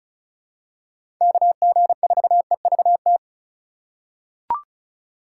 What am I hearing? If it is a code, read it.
Morse code, heard as a keyed tone, about 34 wpm, 700 Hz. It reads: KG4EVT
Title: USA Call Signs - 6 Characters - 4x Character Spacing - 35wpm